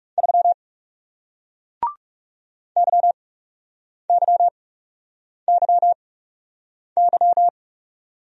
Morse code, heard as a keyed tone, about 40 wpm, spaced as for 8 wpm, 700 Hz. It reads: 3 YYYY